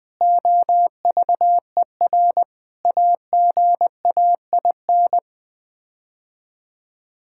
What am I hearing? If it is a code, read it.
Morse code, heard as a keyed tone, about 21 wpm, 700 Hz. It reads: OVER AGAIN
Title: Sets of 6 Words - 20wpm